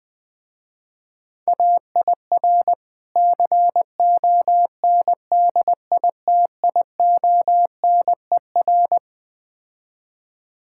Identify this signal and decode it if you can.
Morse code, heard as a keyed tone, about 20 wpm, 700 Hz. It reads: AIR CONDITIONER
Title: Easy Words - Speed-Racing with Hints - 20wpm